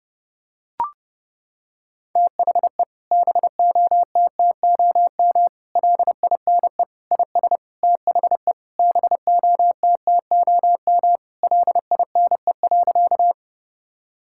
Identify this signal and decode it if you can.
Morse code, heard as a keyed tone, about 32 wpm, 700 Hz. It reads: THE BOTTOM LINE IS THE BOTTOM LINE.
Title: Sentences from Top 1000 Words - Review Part 6 - 30wpm